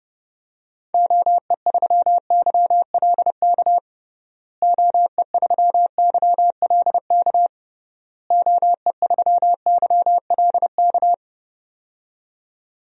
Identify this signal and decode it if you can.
Morse code, heard as a keyed tone, about 30 wpm, 700 Hz. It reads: OE3YLK OE3YLK OE3YLK
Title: DXCC-100 Call Signs - 6 Characters - Repeated 3x - 30wpm